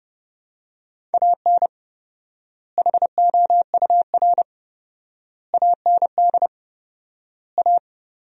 Morse code, heard as a keyed tone, about 30 wpm, 700 Hz. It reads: AN HOUR AND A
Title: 4x Word-Spacing - Sentences from Top 400 Words - 30wpm